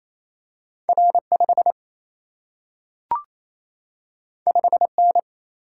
Morse code, heard as a keyed tone, about 28 wpm, 700 Hz. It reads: R5 5N
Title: Single-Letter Single-Number 28wpm